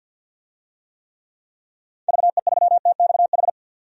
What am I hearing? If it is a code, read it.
Morse code, heard as a keyed tone, about 50 wpm, 700 Hz. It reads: VE3TXH
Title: DXCC-100 Call Signs - 6 Characters - 50wpm